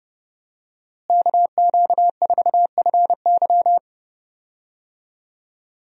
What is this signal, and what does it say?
Morse code, heard as a keyed tone, about 30 wpm, 700 Hz. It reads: KQ4FY
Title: USA Call Signs - 5 Characters - Repeated 2x - 30wpm